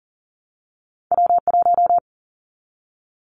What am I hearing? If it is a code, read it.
Morse code, heard as a keyed tone, about 39 wpm, 700 Hz. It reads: W1
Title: Callsign Prefixes for Top 25 DXCC - 40wpm